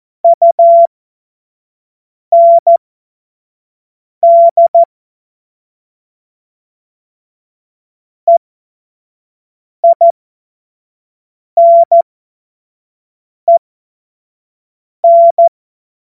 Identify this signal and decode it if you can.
Morse code, heard as a keyed tone, about 14 wpm, spaced as for 5 wpm, 680 Hz. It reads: UND EINEN
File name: Morsekurs / Witze / Witz 21. Februar 2023 25 F70